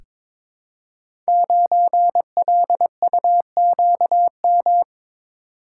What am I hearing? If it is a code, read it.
Morse code, heard as a keyed tone, about 22 wpm, 690 Hz. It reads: 9LUQM
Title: Code Groups - 5 Letter:Numbers - 22wpm